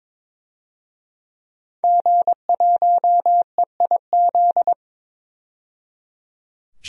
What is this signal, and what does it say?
Morse code, heard as a keyed tone, about 22 wpm, 700 Hz. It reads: G1EIZ
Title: DXCC-100 Call Signs - 5 Characters - 22wpm